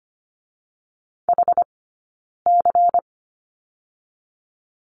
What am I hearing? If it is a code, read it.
Morse code, heard as a keyed tone, about 25 wpm, 700 Hz. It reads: H C
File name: course / 25:10wpm / 092 Lesson 16d - Letter c - Two Characters 25wpm